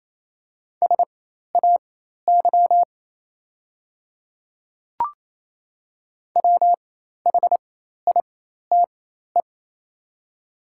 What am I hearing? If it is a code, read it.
Morse code, heard as a keyed tone, about 28 wpm, spaced as for 13 wpm, 700 Hz. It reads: SAY WHITE
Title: 4x Character Spacing - Top 500 Words - 28wpm